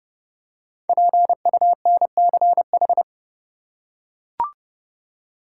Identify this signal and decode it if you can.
Morse code, heard as a keyed tone, about 32 wpm, 700 Hz. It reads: PUNCH
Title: Five Letter Words - 30wpm